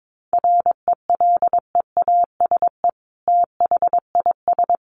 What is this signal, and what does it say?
Morse code, heard as a keyed tone, about 22 wpm, 700 Hz. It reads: RELEASE THIS
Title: Sentences from Top 1100 Words 22wpm